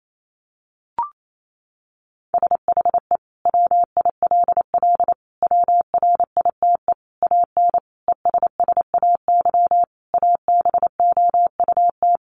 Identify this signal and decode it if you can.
Morse code, heard as a keyed tone, about 28 wpm, 700 Hz. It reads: SHE WILL WRITE AN ESSAY ABOUT
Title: Sentences from Top 1500 Words 28wpm